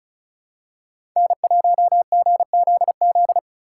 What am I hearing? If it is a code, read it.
Morse code, heard as a keyed tone, about 34 wpm, 700 Hz. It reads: N1GZZ
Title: USA Call Signs - 4x Character Spacing - 35wpm